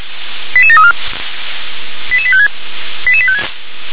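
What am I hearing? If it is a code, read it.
Five-tone selective call (ZVEI), heard as quick runs of five tones, tones of about 70 ms. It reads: 88734 88856 88856
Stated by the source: ZVEI five-tone selective call 88151 received by SDR.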